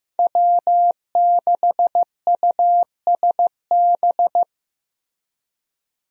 Morse code, heard as a keyed tone, about 16 wpm, 700 Hz. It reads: W6USB
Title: USA Call Signs - 5 Characters - Repeated 2x - 15wpm